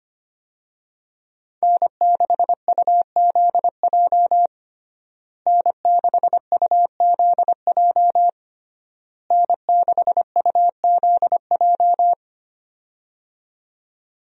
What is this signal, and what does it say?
Morse code, heard as a keyed tone, about 25 wpm, 700 Hz. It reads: N6UZJ N6UZJ N6UZJ
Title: USA Call Signs - 5 Characters - Repeated 3x - 25wpm